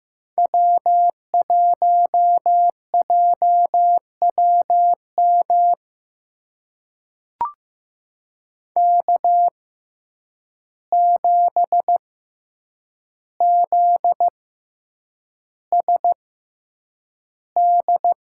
Morse code, heard as a keyed tone, about 15 wpm, spaced as for 8 wpm, 700 Hz. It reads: W1JWM K7ZSD
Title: USA Call Signs - 6x Character Spacing - 15wpm